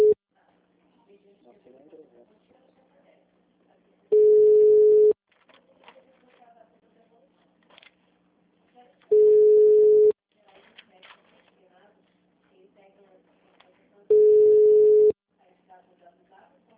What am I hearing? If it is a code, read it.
Ringback tone (Europe, 425 Hz), with speech.